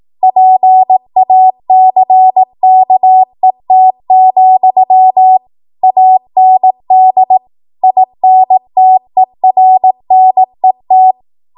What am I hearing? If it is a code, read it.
Morse code, heard as a keyed tone, about 18 wpm, 750 Hz. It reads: PACKET, AND INTERNET
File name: Morse / 18WPM